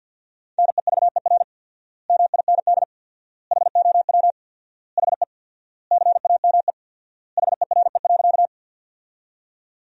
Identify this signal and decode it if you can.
Morse code, heard as a keyed tone, about 48 wpm, 700 Hz. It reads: NEVER MIND HOW HE CAME HERE.
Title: 4x Word-Spacing - Sentences from Top 500 Words - 50wpm